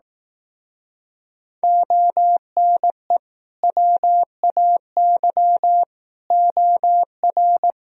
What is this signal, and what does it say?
Morse code, heard as a keyed tone, about 18 wpm, 700 Hz. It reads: ONE WAY OR
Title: Speed-Racing - Sets of 5 Words 15wpm